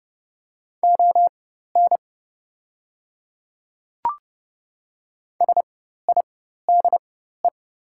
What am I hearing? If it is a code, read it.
Morse code, heard as a keyed tone, about 30 wpm, spaced as for 14 wpm, 690 Hz. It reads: ON SIDE
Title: 4x Character Spacing - Top 100 Words - 30wpm